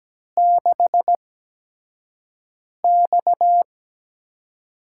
Morse code, heard as a keyed tone, about 17 wpm, spaced as for 5 wpm, 700 Hz. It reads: 6X
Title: USA Call Signs - 8x Character Spacing - 17wpm